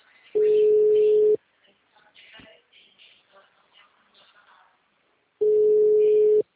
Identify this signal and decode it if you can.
Ringback tone (Europe, 425 Hz), with speech.